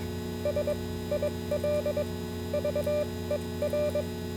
Morse code, heard as a keyed tone, about 22 wpm, spaced as for 16 wpm, 590 Hz, with noise. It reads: SILVER